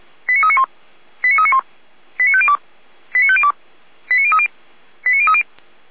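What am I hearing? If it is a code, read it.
Five-tone selective call (ZVEI), heard as quick runs of five tones, tones of about 70 ms. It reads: 89391 89391 89502 89502 89030 89030
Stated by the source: Five-tone ZVEI selective call 89042 received via SDR.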